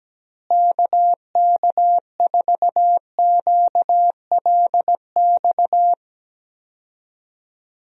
Morse code, heard as a keyed tone, about 17 wpm, 700 Hz. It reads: KK4QLX
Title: USA Call Signs - 6 Characters - Repeated 3x - 17wpm